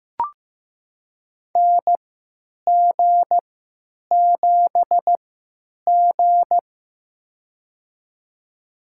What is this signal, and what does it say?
Morse code, heard as a keyed tone, about 15 wpm, spaced as for 9 wpm, 700 Hz. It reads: NG7G
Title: USA Call Signs - 4 Characters - 3x Character Spacing - 15wpm